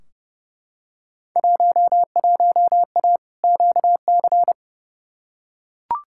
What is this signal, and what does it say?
Morse code, heard as a keyed tone, about 30 wpm, 700 Hz. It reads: 11A QC
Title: Field Day Exchanges Encore - 30wpm